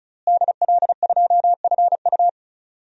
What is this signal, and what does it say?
Morse code, heard as a keyed tone, about 36 wpm, 700 Hz. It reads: DL2FU
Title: DXCC-100 Call Signs - 5 Characters - Repeated 2x - 35wpm